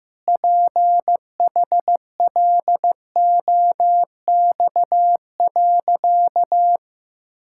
Morse code, heard as a keyed tone, about 16 wpm, 700 Hz. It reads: PHLOX.
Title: Pangram Challenge v2 - 15wpm